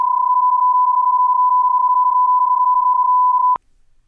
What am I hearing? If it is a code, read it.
PSK31, about 31.25 baud.